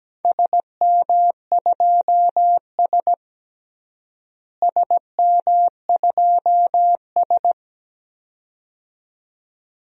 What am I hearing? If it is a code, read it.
Morse code, heard as a keyed tone, about 17 wpm, 700 Hz. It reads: SM2S SM2S
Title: DXCC-100 Call Signs - 4 Characters - Repeated 2x - 17wpm